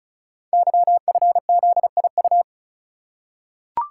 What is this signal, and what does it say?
Morse code, heard as a keyed tone, about 36 wpm, 700 Hz. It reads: YFZIU
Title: Code Groups - 5 Letters - 35wpm